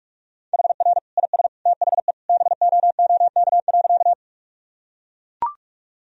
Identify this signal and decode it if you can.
Morse code, heard as a keyed tone, about 41 wpm, 700 Hz. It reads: HR IS THE BOOK.
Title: Sentences from Top 300 Words and Abbreviations 45wpm